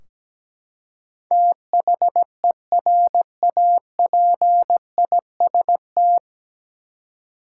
Morse code, heard as a keyed tone, about 17 wpm, 700 Hz. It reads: THERAPIST
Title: K1USN SST QSO Practice 17wpm